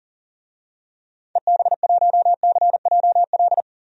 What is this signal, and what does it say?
Morse code, heard as a keyed tone, about 40 wpm, 700 Hz. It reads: EB1CJL
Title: DXCC-100 Call Signs - 6 Characters - 40wpm